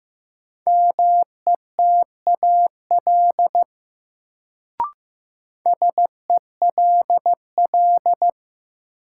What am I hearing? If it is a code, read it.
Morse code, heard as a keyed tone, about 15 wpm, 700 Hz. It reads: METAL SELL
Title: Sentences from Top 1000 Words - Review Part 6 - New Word Drill - 15wpm